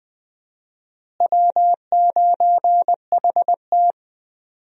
Morse code, heard as a keyed tone, about 20 wpm, 700 Hz. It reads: W9HT
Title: USA Call Signs - 4 Characters - Repeated 3x - 20wpm